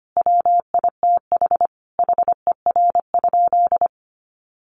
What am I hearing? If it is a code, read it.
Morse code, heard as a keyed tone, about 25 wpm, 700 Hz. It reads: WITH HER?
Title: Sentences from Top 400 Words 25wpm